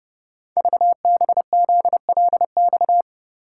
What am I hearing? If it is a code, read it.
Morse code, heard as a keyed tone, about 30 wpm, 700 Hz. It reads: VBZLX
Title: Code Groups - 5 Letters - 30wpm